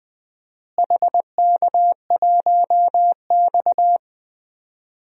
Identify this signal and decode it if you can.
Morse code, heard as a keyed tone, about 20 wpm, 700 Hz. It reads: HK1X
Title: DXCC-100 Call Signs - 4 Characters - Repeated 3x - 20wpm